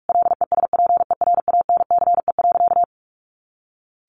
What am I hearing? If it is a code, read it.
Morse code, heard as a keyed tone, about 44 wpm, 700 Hz. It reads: LESPERANCE.
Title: A Sound Of Thunder 45wpm